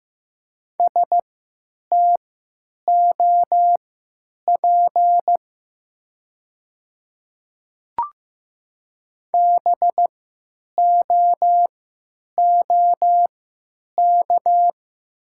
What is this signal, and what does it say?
Morse code, heard as a keyed tone, about 15 wpm, spaced as for 9 wpm, 700 Hz. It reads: STOP BOOK
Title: 3x Character Spacing - Top 300 Words - 15wpm